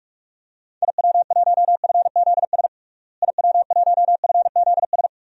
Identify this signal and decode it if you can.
Morse code, heard as a keyed tone, about 41 wpm, 700 Hz. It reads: IW1FZS IW1FZS
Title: DXCC-100 Call Signs - 6 Characters - Repeated 2x - 45wpm